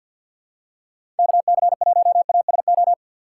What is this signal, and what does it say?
Morse code, heard as a keyed tone, about 53 wpm, 700 Hz. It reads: KC1ASO
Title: USA Call Signs - 6x Character Spacing - 50wpm